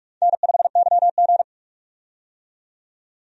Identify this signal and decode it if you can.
Morse code, heard as a keyed tone, about 44 wpm, 700 Hz. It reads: N5YG
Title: USA Call Signs - 4 Characters - 45wpm